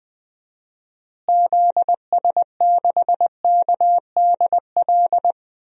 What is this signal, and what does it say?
Morse code, heard as a keyed tone, about 20 wpm, 700 Hz. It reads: ZS6KDL
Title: DXCC-100 Call Signs - 6 Characters - Repeated 2x - 20wpm